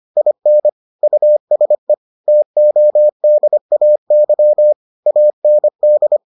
Morse code, heard as a keyed tone, about 25 wpm, 600 Hz. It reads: IN USE TODAY AND